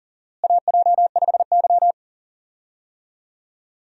Morse code, heard as a keyed tone, about 39 wpm, 700 Hz. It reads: AJ5Y
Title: USA Call Signs - 4 Characters - Repeated 3x - 40wpm